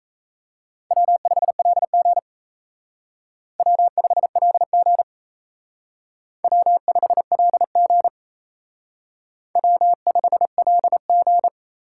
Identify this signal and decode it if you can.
Morse code, heard as a keyed tone, about 34 wpm, 700 Hz. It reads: W5LG W5LG W5LG W5LG
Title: Speed Racing - USA Call Signs 28wpm